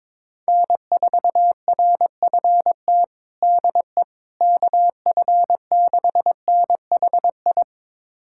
Morse code, heard as a keyed tone, about 22 wpm, 700 Hz. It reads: N4RFT DE KF6NHI
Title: USA Callsign DE Callsign 22wpm